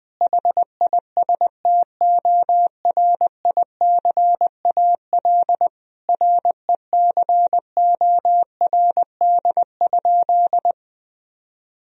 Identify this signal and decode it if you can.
Morse code, heard as a keyed tone, about 20 wpm, 700 Hz. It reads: HISTORICAL RECORD?
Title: Sentences from Top 1000 Words - Review Part 4 - 20wpm